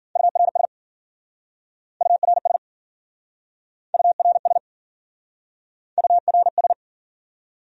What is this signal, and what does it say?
Morse code, heard as a keyed tone, about 52 wpm, 700 Hz. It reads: URS URS URS URS